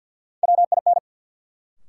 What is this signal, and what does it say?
Morse code, heard as a keyed tone, about 52 wpm, 700 Hz. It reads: WIN
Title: Sentences from Top 1500 Words - New Word Drill - 50wpm